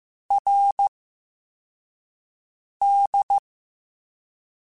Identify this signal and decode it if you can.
Morse code, heard as a keyed tone, about 15 wpm, 800 Hz. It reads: R D
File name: Special INGs-mp3s / Special INGs-mp3s / SlowToFaster ing @10